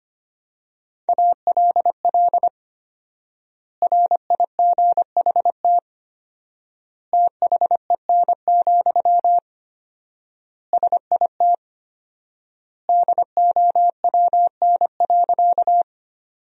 Morse code, heard as a keyed tone, about 25 wpm, 700 Hz. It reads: ALL RIGHT THEN, SIT DOWN.
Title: 4x Word-Spacing - Sentences from Top 500 Words - 25wpm